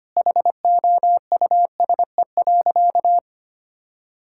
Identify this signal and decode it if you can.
Morse code, heard as a keyed tone, about 25 wpm, 700 Hz. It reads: HOUSE.
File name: practice / 25wpm / Sentences with 2-Word Phrases - 25wpm